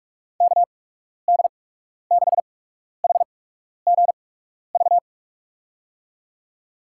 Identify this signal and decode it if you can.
Morse code, heard as a keyed tone, about 42 wpm, spaced as for 12 wpm, 700 Hz. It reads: KD6HGV
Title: USA Call Signs - 6 Characters - 8x Character Spacing - 45wpm